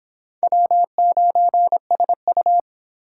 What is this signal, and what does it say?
Morse code, heard as a keyed tone, about 24 wpm, 700 Hz. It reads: W9SU